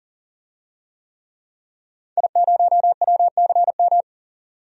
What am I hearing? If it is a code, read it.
Morse code, heard as a keyed tone, about 40 wpm, 700 Hz. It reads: I0WCM
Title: DXCC-100 Call Signs - 5 Characters - 40wpm